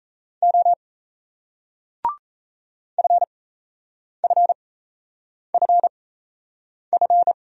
Morse code, heard as a keyed tone, about 37 wpm, spaced as for 8 wpm, 700 Hz. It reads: O FFFF